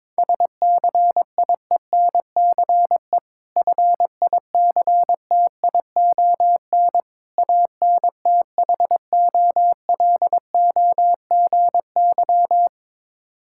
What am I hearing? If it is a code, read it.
Morse code, heard as a keyed tone, about 22 wpm, 690 Hz. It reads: SCIENCE FICTION ANTHOLOGY
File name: Difficult Words with Hints - 22wpm